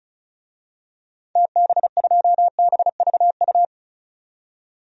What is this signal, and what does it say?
Morse code, heard as a keyed tone, about 34 wpm, 700 Hz. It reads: TB2BVU
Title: DXCC-100 Call Signs - 6 Characters - 35wpm